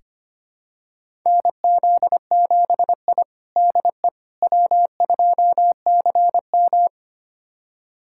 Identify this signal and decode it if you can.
Morse code, heard as a keyed tone, about 25 wpm, 700 Hz. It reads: NZ7I DE W2CM
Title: USA Callsign DE Callsign - 4 Characters - 25wpm